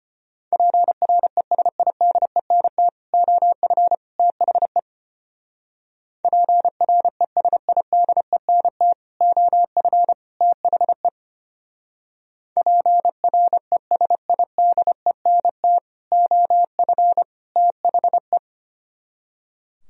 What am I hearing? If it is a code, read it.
Morse code, heard as a keyed tone, about 32 wpm, 700 Hz. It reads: PRESIDENT OF THE PRESIDENT OF THE PRESIDENT OF THE